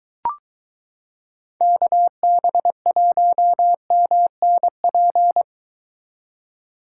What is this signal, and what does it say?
Morse code, heard as a keyed tone, about 22 wpm, 700 Hz. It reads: KB1MNP